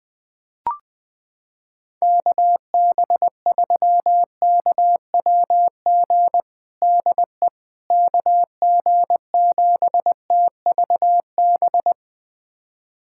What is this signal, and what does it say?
Morse code, heard as a keyed tone, about 20 wpm, 700 Hz. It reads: KB3KWG DE KG7TVB
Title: USA Callsign DE Callsign 20wpm